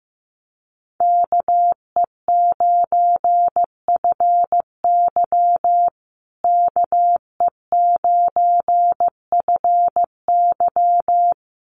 Morse code, heard as a keyed tone, about 15 wpm, 700 Hz. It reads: KE9FY KE9FY
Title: USA Call Signs Repeated 2x 15wpm